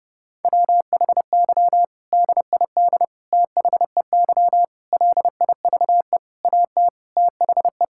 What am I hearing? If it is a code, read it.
Morse code, heard as a keyed tone, about 30 wpm, 700 Hz. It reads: WHY DID THEY LIVE AT THE
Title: Sentences from Top 1000 Words - Review Part 6 - 30wpm